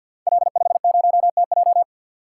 Morse code, heard as a keyed tone, about 47 wpm, 700 Hz. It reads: L50TJ
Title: Code Groups - 5 Letter:Numbers - 50wpm